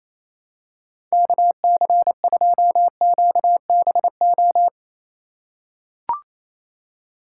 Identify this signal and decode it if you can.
Morse code, heard as a keyed tone, about 28 wpm, 700 Hz. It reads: KC2QBO
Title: USA Call Signs - 6 Characters - 4x Character Spacing - 28wpm